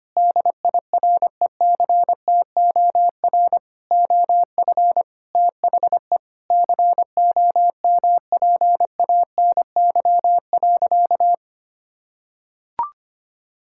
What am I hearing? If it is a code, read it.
Morse code, heard as a keyed tone, about 25 wpm, 700 Hz. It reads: DIRECTOR OF THE COMPANY.
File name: practice / 25wpm / Sentences with 3-Word Phrases - 25wpm